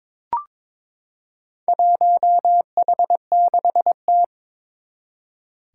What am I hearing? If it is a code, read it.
Morse code, heard as a keyed tone, about 22 wpm, 700 Hz. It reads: 1H6T
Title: Four Challenging Letter-Number Combinations 22wpm